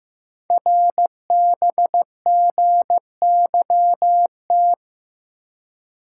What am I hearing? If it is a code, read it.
Morse code, heard as a keyed tone, about 15 wpm, 700 Hz. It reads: RBGYT
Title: Code Groups - 5 Letters - 15wpm